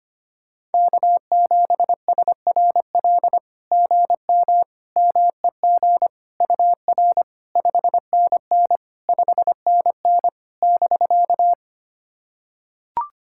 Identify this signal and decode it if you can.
Morse code, heard as a keyed tone, about 25 wpm, 700 Hz. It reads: K7SRL GM MEG UR 5NN 5NN <BK>
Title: SOTA QSO Practice 25wpm